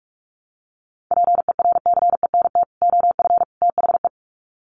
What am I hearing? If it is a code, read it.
Morse code, heard as a keyed tone, about 44 wpm, 700 Hz. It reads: PERCENT OF THE
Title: Sets of 3 Words 45wpm